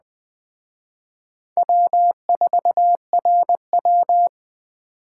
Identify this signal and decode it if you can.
Morse code, heard as a keyed tone, about 20 wpm, 700 Hz. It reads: W4RW
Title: Speed Racing - USA 1x2 Call Signs 17wpm